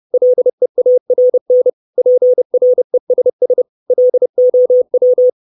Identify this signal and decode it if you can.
Morse code, heard as a keyed tone, about 30 wpm, 500 Hz. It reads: LEARN PRESS LOW
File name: 632 Words 30WPM